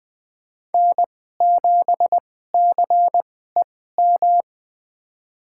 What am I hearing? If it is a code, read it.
Morse code, heard as a keyed tone, about 20 wpm, spaced as for 14 wpm, 700 Hz. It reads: N7CEM
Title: USA Call Signs - 5 Characters - 2x Character Spacing - 20wpm